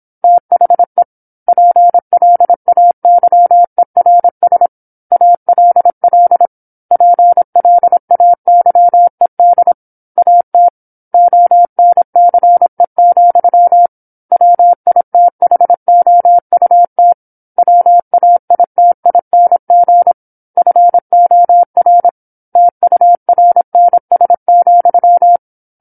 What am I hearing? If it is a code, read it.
Morse code, heard as a keyed tone, about 27 wpm, 700 Hz. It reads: THE PLAYERS ALL PLAYED AT ONCE, WITHOUT WAITING FOR TURNS,
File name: 08-Chapter-08 26 WPM